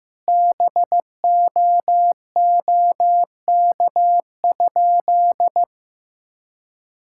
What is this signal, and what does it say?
Morse code, heard as a keyed tone, about 16 wpm, 700 Hz. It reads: BOOK?
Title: Sentences from Top 1000 Words - Review Part 5 - 15wpm